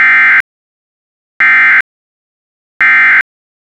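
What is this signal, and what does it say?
Emergency Alert System (SAME) tones, heard as harsh buzzing bursts. It is heard from 0.0 to 3.2 s.
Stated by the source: EAS End of Message burst from a Required Weekly Test, originator KR8MER.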